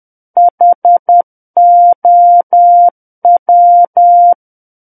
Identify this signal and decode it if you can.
Morse code, heard as a keyed tone, about 10 wpm, 700 Hz. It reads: HOW
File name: Book One / 01-Chapter 1 - 10 WPM